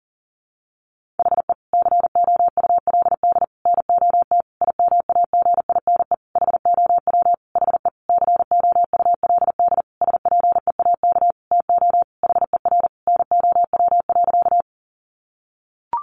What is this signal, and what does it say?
Morse code, heard as a keyed tone, about 40 wpm, 700 Hz. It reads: HE COULD NOT IMAGINE HOW HE COULD SPEAK TO HER NOW.
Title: Sentences from Top 1400 Words 40wpm